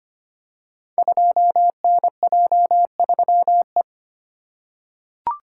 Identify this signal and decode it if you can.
Morse code, heard as a keyed tone, about 25 wpm, 700 Hz. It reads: 2NJ3E
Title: Code Groups - 5 Letter:Numbers - 25wpm